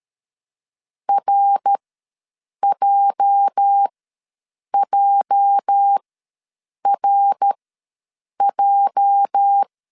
Morse code, heard as a keyed tone, about 13 wpm, spaced as for 7 wpm, 800 Hz. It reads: RJJRJ
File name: LEZIONI / 8 YJ45 / 8 YJ ws 12wpm 3342